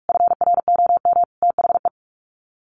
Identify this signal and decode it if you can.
Morse code, heard as a keyed tone, about 44 wpm, 700 Hz. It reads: FROM THE
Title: Sets of 5 Words Encore2 45wpm